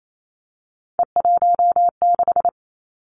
Morse code, heard as a keyed tone, about 28 wpm, 700 Hz. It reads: E16
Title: Three Challenging Letter-Number Combinations 28wpm